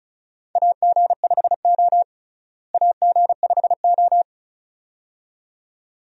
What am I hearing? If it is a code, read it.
Morse code, heard as a keyed tone, about 36 wpm, 700 Hz. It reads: AG5O AG5O
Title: USA Call Signs - 4 Characters - Repeated 2x - 35wpm